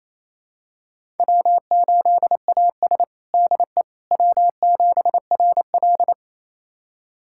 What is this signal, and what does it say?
Morse code, heard as a keyed tone, about 28 wpm, 700 Hz. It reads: W8AS DE W7RL
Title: USA Callsign DE Callsign - 4 Characters - 28wpm